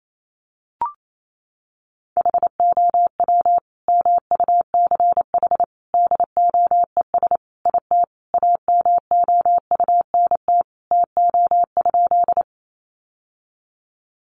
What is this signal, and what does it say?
Morse code, heard as a keyed tone, about 28 wpm, 700 Hz. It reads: HOW MUCH DOES IT AMOUNT TO?
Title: Sentences from Top 700 Words 28wpm